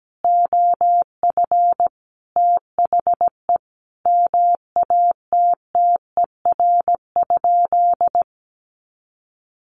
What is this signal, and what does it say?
Morse code, heard as a keyed tone, about 17 wpm, 700 Hz. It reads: OF THE MATTER?
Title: Sentences from Top 1000 Words 17wpm